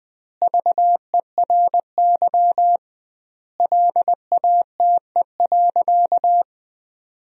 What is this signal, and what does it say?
Morse code, heard as a keyed tone, about 21 wpm, 700 Hz. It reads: VERY LATE.
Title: 2x Word-Spacing - Sentences from Top 300 Words - 20wpm